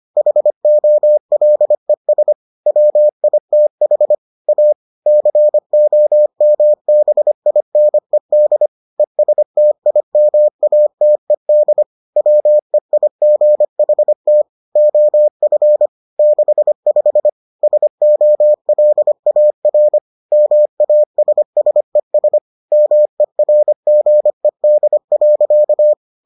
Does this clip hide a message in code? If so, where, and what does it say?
Yes: Morse code from 0.2 to 25.9 s, heard as a keyed tone, about 25 wpm, 600 Hz. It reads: HOLES WITH A COMBINED ESTIMATED WEIGHT OF 65 SOLAR MASSES MERGED.